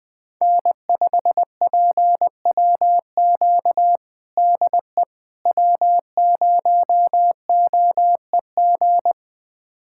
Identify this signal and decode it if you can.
Morse code, heard as a keyed tone, about 20 wpm, 700 Hz. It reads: N5PWQ DE W0OEG
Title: USA Callsign DE Callsign - 5 Characters - 20wpm